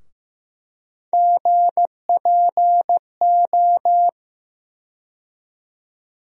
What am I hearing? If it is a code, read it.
Morse code, heard as a keyed tone, about 15 wpm, 700 Hz. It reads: GPO
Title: USA Call Sign Postfixes - 3 Letters 15wpm